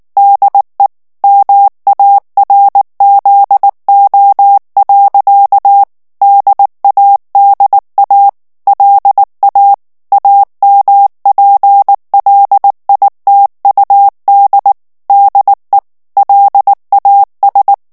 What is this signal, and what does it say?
Morse code, heard as a keyed tone, about 19 wpm, 800 Hz. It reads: DE MARZO. DADA LA AMPLITUD DE LAS
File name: morse mp3 / ESPAGNOL TEXTE EN CLAIR 2 VITESSE 18 MOTS MINUTES